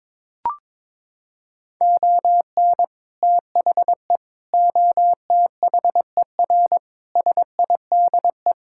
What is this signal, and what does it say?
Morse code, heard as a keyed tone, about 22 wpm, 700 Hz. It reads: ON THE OTHER SIDE
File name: practice / 22wpm / Sentences with 2-Word Phrases - 22wpm